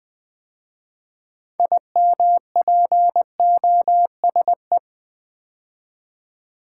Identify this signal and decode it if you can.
Morse code, heard as a keyed tone, about 20 wpm, 700 Hz. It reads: IMPOSE
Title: 6 Letter Words - First 3 Letters Given - 20wpm